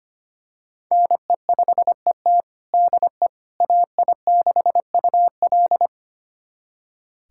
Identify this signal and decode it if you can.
Morse code, heard as a keyed tone, about 25 wpm, 700 Hz. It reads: NE5ET DE AI6UL
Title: USA Callsign DE Callsign - 5 Characters - 25wpm